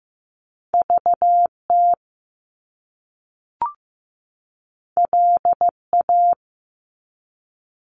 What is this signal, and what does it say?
Morse code, heard as a keyed tone, about 15 wpm, 700 Hz. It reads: VT LA
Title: US States 15wpm